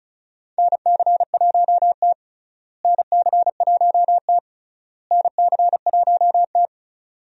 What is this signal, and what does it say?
Morse code, heard as a keyed tone, about 36 wpm, 700 Hz. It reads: NC1T NC1T NC1T
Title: USA Call Signs - 4 Characters - Repeated 3x - 35wpm